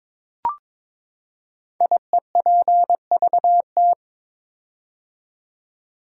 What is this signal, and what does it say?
Morse code, heard as a keyed tone, about 22 wpm, 700 Hz. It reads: IEPVT
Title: Code Groups - 5 Letters - 22wpm